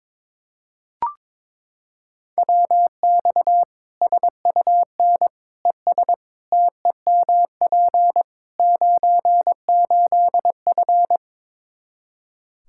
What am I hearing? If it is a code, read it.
Morse code, heard as a keyed tone, about 22 wpm, 700 Hz. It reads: WX SUN ES TEMP 98F
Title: K1USN SST QSO Practice 22wpm